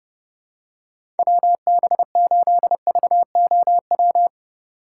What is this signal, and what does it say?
Morse code, heard as a keyed tone, about 30 wpm, 700 Hz. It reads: WB8VOW
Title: USA Call Signs - 6 Characters - 3x Character Spacing - 30wpm